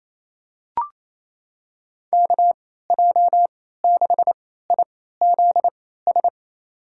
Morse code, heard as a keyed tone, about 28 wpm, spaced as for 16 wpm, 700 Hz. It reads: KJ6IZS
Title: USA Call Signs - 3x Character Spacing - 28wpm